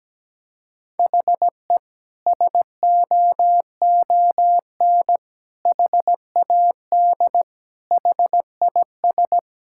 Morse code, heard as a keyed tone, about 17 wpm, 700 Hz. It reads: HE SOON HAD HIS
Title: Sentences from Top 1600 Words 17wpm